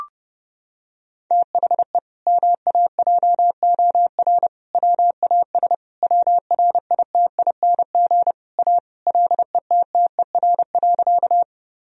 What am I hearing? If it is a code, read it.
Morse code, heard as a keyed tone, about 30 wpm, 700 Hz. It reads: THE MAJOR WAS WRITING A LETTER.
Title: Sentences from Top 1000 Words - Review Part 4 - 30wpm